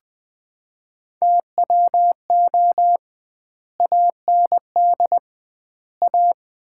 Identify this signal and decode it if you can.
Morse code, heard as a keyed tone, about 21 wpm, 700 Hz. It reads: TWO AND A
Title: 2x Word-Spacing - Sets of 5 Words - 20wpm